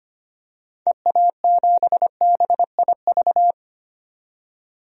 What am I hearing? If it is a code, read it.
Morse code, heard as a keyed tone, about 25 wpm, 700 Hz. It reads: EA7BIV
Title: DXCC-100 Call Signs - 6 Characters - 25wpm